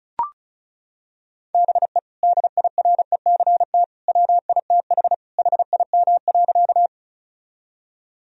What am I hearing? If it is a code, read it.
Morse code, heard as a keyed tone, about 34 wpm, 700 Hz. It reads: BE DIRECT WITH HIM.
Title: Sentences from Top 1000 Words - Review Part 2 - 35wpm